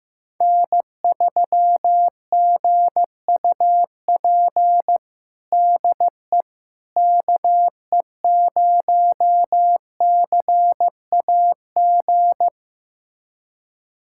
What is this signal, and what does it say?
Morse code, heard as a keyed tone, about 16 wpm, 700 Hz. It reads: N3GUP DE KE0CAG
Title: USA Callsign DE Callsign 15wpm